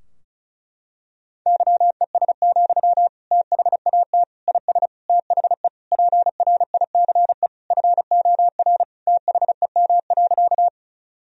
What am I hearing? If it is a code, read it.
Morse code, heard as a keyed tone, about 36 wpm, 700 Hz. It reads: YES, THAT IS THE PRICE FOR THEM.
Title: Sentences from Top 1000 Words - Review Part 4 - 35wpm